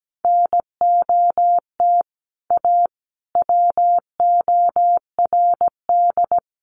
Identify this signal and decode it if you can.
Morse code, heard as a keyed tone, about 17 wpm, 700 Hz. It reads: NOT A WORD
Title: Sentences from Top 800 Words 17wpm